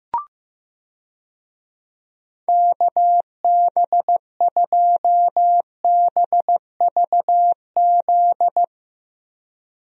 Morse code, heard as a keyed tone, about 15 wpm, 700 Hz. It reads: KB2BVZ
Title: USA Call Signs - 6 Characters - Repeated 2x - 15wpm